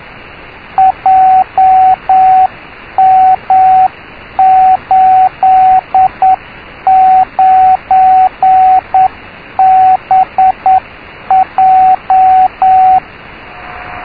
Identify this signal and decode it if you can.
Morse code, heard as a keyed tone, about 9 wpm, 760 Hz, with radio noise. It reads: JM89BJ